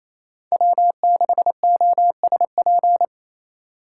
Morse code, heard as a keyed tone, about 27 wpm, 700 Hz. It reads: W6OSP
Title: K1USN SST QSO Practice - 28wpm